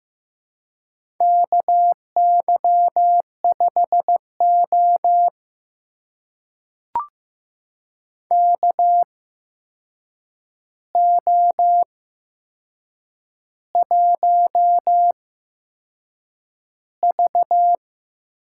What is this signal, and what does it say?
Morse code, heard as a keyed tone, about 15 wpm, spaced as for 6 wpm, 700 Hz. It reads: KY5O KO1V
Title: USA Call Signs - 4 Characters - 8x Character Spacing - 15wpm